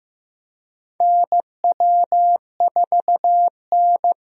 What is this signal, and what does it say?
Morse code, heard as a keyed tone, about 15 wpm, 700 Hz. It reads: NW4N
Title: USA Call Signs - 4 Characters - Repeated 2x - 15wpm